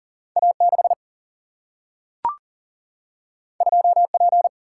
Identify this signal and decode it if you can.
Morse code, heard as a keyed tone, about 40 wpm, 700 Hz. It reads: A6 2P
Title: Single-Letter Single-Number 40wpm